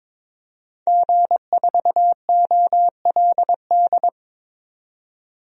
Morse code, heard as a keyed tone, about 22 wpm, 700 Hz. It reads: G4OLD
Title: DXCC-100 Call Signs - 5 Characters - 22wpm